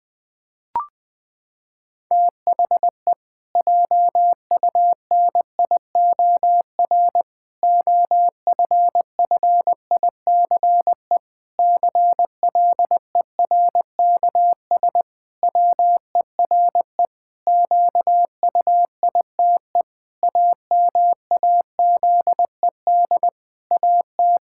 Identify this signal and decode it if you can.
Morse code, heard as a keyed tone, about 20 wpm, 700 Hz. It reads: THE JUNIOR OFFICE CLERKS WERE QUITE AMAZED AT
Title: Pangram Challenge v2 - 20wpm